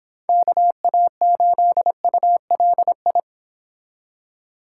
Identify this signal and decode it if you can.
Morse code, heard as a keyed tone, about 26 wpm, 700 Hz. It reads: KA8ULI